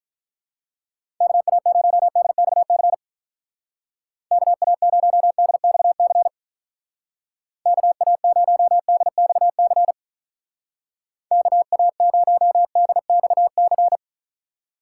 Morse code, heard as a keyed tone, about 46 wpm, 700 Hz. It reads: KA0DXC KA0DXC KA0DXC KA0DXC